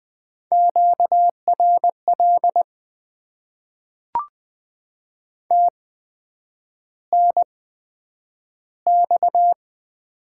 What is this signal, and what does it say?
Morse code, heard as a keyed tone, about 20 wpm, spaced as for 9 wpm, 700 Hz. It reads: QRL TNX
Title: Top 90 QSO Elements - 8x Character Spacing - 20wpm